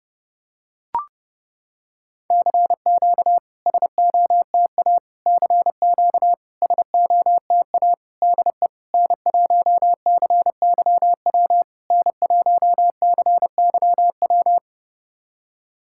Morse code, heard as a keyed tone, about 32 wpm, 690 Hz. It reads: CQ SOTA CQ SOTA DE N1CYW N1CYW
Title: SOTA QSO Practice 30wpm